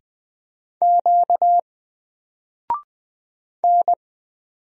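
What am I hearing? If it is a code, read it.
Morse code, heard as a keyed tone, about 20 wpm, 700 Hz. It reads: Q N
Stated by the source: Single letter and number Morse drill at 20 wpm, from N4LCD.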